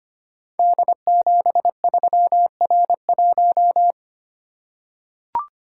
Morse code, heard as a keyed tone, about 25 wpm, 700 Hz. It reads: D73R1
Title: Code Groups - 5 Letter:Numbers - 25wpm